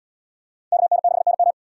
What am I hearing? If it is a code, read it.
Morse code, heard as a keyed tone, about 72 wpm, 690 Hz. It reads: KN6NM